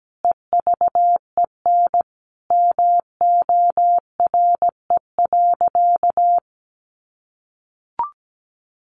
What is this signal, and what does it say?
Morse code, heard as a keyed tone, about 17 wpm, 700 Hz. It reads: EVEN MORE.
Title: Sentences from Top 1300 Words 17wpm